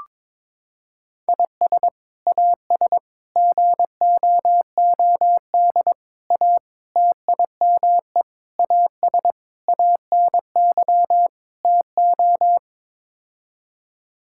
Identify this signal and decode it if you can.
Morse code, heard as a keyed tone, about 22 wpm, 700 Hz. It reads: IS AS GOOD A TIME AS ANY TO
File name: practice / 22wpm / Sets of 8 Words - 22wpm